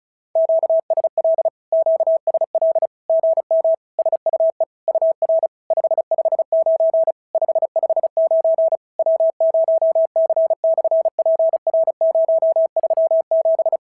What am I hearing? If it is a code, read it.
Morse code, heard as a keyed tone, about 36 wpm, 650 Hz. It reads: QSL QSL GM SUE UR 559 559 W0C/PR037
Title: SOTA QSO Practice 35wpm